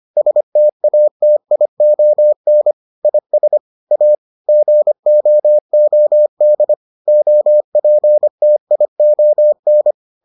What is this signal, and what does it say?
Morse code, heard as a keyed tone, about 25 wpm, 600 Hz. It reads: STATION IS A GOOD OPTION